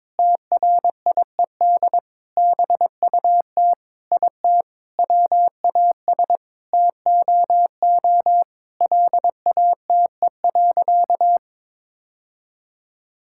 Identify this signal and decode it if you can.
Morse code, heard as a keyed tone, about 22 wpm, 700 Hz. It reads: TRIED BUT IT WAS TOO LATE.
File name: practice / 22wpm / Sentences with 3-Word Phrases - 22wpm